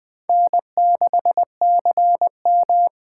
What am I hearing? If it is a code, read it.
Morse code, heard as a keyed tone, about 20 wpm, 700 Hz. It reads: N6CM